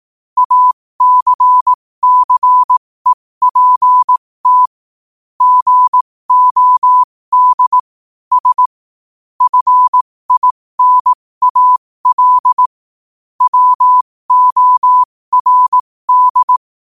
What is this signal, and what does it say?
Morse code, heard as a keyed tone, about 18 wpm, spaced as for 15 wpm, 1000 Hz. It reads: ACCEPT GOD S FINAL WORD